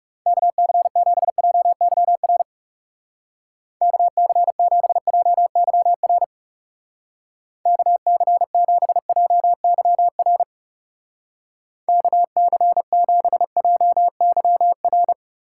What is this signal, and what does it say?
Morse code, heard as a keyed tone, about 38 wpm, 700 Hz. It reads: KC7JYR KC7JYR KC7JYR KC7JYR